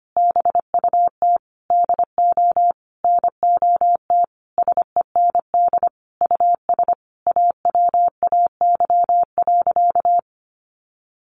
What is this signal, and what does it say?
Morse code, heard as a keyed tone, about 25 wpm, 700 Hz. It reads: BUT DO NOT SEND US AWAY.
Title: Sentences from Top 1500 Words 25wpm